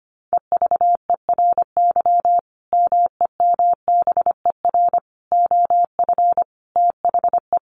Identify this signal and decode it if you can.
Morse code, heard as a keyed tone, about 25 wpm, 700 Hz. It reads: EVERY MEMBER OF THE
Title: Sentences from Top 1400 Words 25wpm